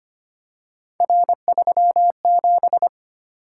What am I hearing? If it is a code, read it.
Morse code, heard as a keyed tone, about 25 wpm, 700 Hz. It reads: R37
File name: Three Letter-Number Combinations - 25wpm